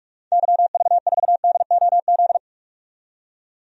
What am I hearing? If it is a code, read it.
Morse code, heard as a keyed tone, about 46 wpm, 700 Hz. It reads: YV4DOZ
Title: DXCC-100 Call Signs - 6 Characters - 45wpm